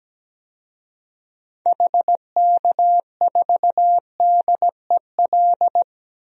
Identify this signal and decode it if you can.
Morse code, heard as a keyed tone, about 17 wpm, 700 Hz. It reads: HK4DEL
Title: DXCC-100 Call Signs - 6 Characters - 17wpm